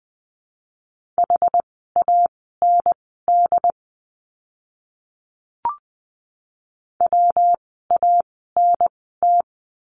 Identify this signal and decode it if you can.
Morse code, heard as a keyed tone, about 20 wpm, spaced as for 14 wpm, 700 Hz. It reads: HAND WANT
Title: 206 Lesson 33f - Letter j - Words 20wpm